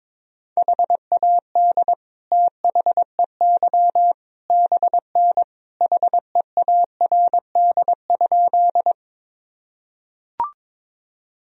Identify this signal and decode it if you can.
Morse code, heard as a keyed tone, about 22 wpm, 700 Hz. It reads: HAD THEY BN HEARD?
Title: Sentences from Top 400 Words and Abbreviations 22wpm